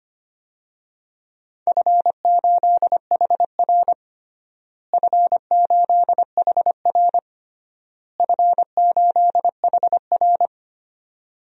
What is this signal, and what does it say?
Morse code, heard as a keyed tone, about 25 wpm, 700 Hz. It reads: F8HR F8HR F8HR
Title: DXCC-100 Call Signs - 4 Characters - Repeated 3x - 25wpm